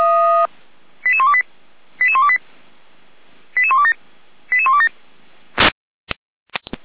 Five-tone selective call (ZVEI), heard as quick runs of five tones, tones of about 70 ms. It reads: 88128 88128 88127 88127